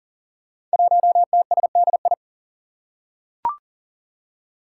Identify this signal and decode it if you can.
Morse code, heard as a keyed tone, about 37 wpm, 700 Hz. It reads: 1TSDI
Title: Five Challenging Letter-Number Combinations 40wpm